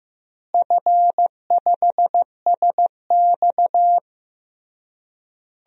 Morse code, heard as a keyed tone, about 15 wpm, 700 Hz. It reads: F5SX
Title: DXCC-100 Call Signs - 4 Characters - Repeated 3x - 15wpm